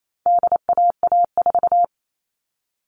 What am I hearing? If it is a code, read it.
Morse code, heard as a keyed tone, about 28 wpm, 700 Hz. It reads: DAA4
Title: All Cut Numbers - 4-Digit 28wpm